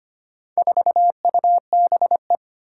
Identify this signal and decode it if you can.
Morse code, heard as a keyed tone, about 25 wpm, 700 Hz. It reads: 4UBE
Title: Four Challenging Letter-Number Combinations 25wpm